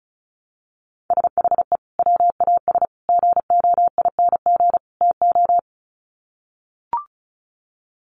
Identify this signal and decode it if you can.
Morse code, heard as a keyed tone, about 34 wpm, 700 Hz. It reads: SHE WAS GOING TO
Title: Sets of 4 Words 35wpm